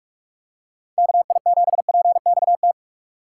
Morse code, heard as a keyed tone, about 44 wpm, 700 Hz. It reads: KI7PXT
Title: USA Call Signs - 2x Character Spacing - 45wpm